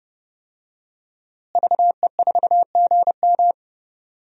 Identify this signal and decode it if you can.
Morse code, heard as a keyed tone, about 32 wpm, 700 Hz. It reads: VE4GM
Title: DXCC-100 Call Signs - 5 Characters - Repeated 3x - 30wpm